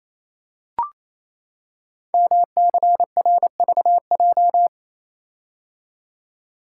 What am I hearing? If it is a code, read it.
Morse code, heard as a keyed tone, about 28 wpm, 690 Hz. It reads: MCRVJ
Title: Code Groups - 5 Letters - 28wpm